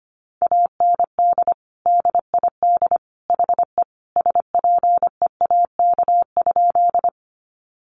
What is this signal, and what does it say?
Morse code, heard as a keyed tone, about 25 wpm, 700 Hz. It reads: AND DID HE SPEAK?
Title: Sentences from Top 1100 Words 25wpm